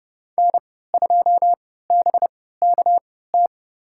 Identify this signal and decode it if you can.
Morse code, heard as a keyed tone, about 30 wpm, spaced as for 17 wpm, 700 Hz. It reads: N2BKT
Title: USA Call Signs - 5 Characters - 3x Character Spacing - 30wpm